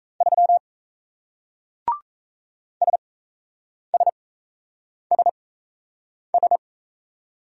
Morse code, heard as a keyed tone, about 35 wpm, spaced as for 8 wpm, 700 Hz. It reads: 3 SSSS
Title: Speed-Racing Single Letter-Number 28wpm